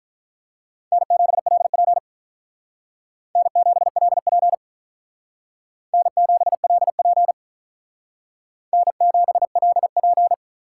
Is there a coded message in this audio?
Yes: Morse code from 0.9 to 10.3 s, heard as a keyed tone, about 44 wpm, 700 Hz. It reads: N7LP N7LP N7LP N7LP